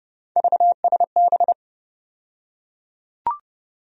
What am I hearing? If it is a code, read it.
Morse code, heard as a keyed tone, about 30 wpm, 700 Hz. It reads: VSB
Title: USA Call Sign Postfixes - 3 Letters 30wpm